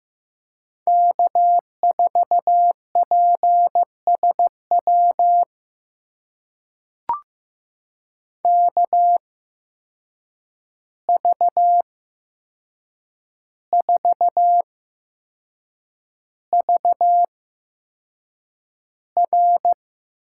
Morse code, heard as a keyed tone, about 15 wpm, spaced as for 6 wpm, 700 Hz. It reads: K4PSW KV4VR
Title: USA Call Signs - 5 Characters - 8x Character Spacing - 15wpm